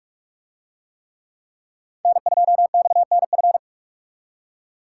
Morse code, heard as a keyed tone, about 44 wpm, 700 Hz. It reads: N2XNF